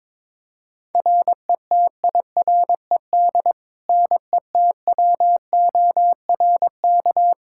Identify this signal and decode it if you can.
Morse code, heard as a keyed tone, about 22 wpm, 700 Hz. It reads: RETIRED NETWORK
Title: Practice QSO - Exchange 3 - More Info - 22wpm